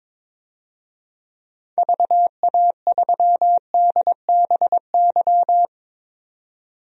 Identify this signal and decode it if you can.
Morse code, heard as a keyed tone, about 22 wpm, 700 Hz. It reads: VA3DBY
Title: DXCC-100 Call Signs - 6 Characters - 22wpm